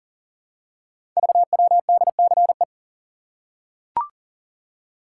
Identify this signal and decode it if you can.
Morse code, heard as a keyed tone, about 38 wpm, 700 Hz. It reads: VWDCE
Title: Code Groups - 5 Letters - 40wpm